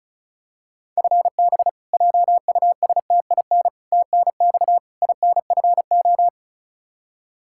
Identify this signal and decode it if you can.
Morse code, heard as a keyed tone, about 34 wpm, 700 Hz. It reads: FB JUSTIN TNX INFO
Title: Practice QSO - Exchange 3 - More Info - 35wpm